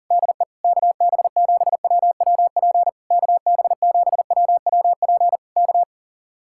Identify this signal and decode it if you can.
Morse code, heard as a keyed tone, about 40 wpm, 700 Hz. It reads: DE KB7WWP KB7WWP K
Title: Speed-Racing Common QSO Phrases Encore - 40wpm